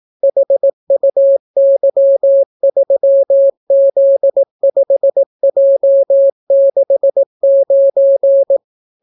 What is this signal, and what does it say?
Morse code, heard as a keyed tone, about 18 wpm, 550 Hz. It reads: HUY3Z5J69